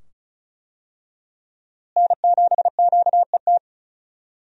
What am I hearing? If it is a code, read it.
Morse code, heard as a keyed tone, about 36 wpm, 700 Hz. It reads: N7QET
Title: USA Call Signs - 5 Characters - Repeated 2x - 35wpm